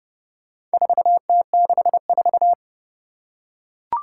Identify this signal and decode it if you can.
Morse code, heard as a keyed tone, about 28 wpm, 700 Hz. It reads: 4T64
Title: Four Challenging Letter-Number Combinations 30wpm